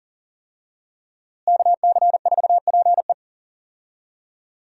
Morse code, heard as a keyed tone, about 40 wpm, 700 Hz. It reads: KC4PE
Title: USA Call Signs - 5 Characters - Repeated 2x - 40wpm